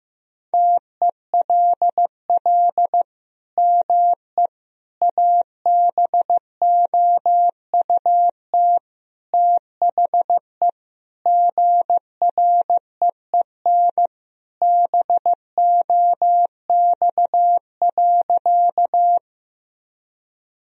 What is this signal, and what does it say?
Morse code, heard as a keyed tone, about 15 wpm, 700 Hz. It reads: TELL ME ABOUT THE GREEN BOX.
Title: Sentences from Top 1000 Words - Review Part 2 - 15wpm